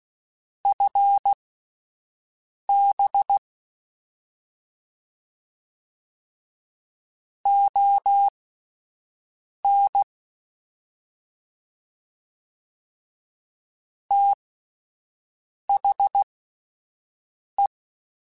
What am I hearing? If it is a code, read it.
Morse code, heard as a keyed tone, about 16 wpm, spaced as for 5 wpm, 800 Hz. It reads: FB ON THE
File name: QSOs / QSOs-mp3s / WD0FBT 1@10